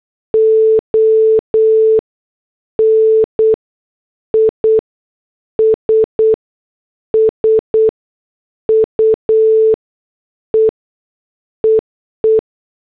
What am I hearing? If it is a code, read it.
Morse code, heard as a keyed tone, about 8 wpm, spaced as for 6 wpm, 440 Hz. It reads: ON ISSUE EE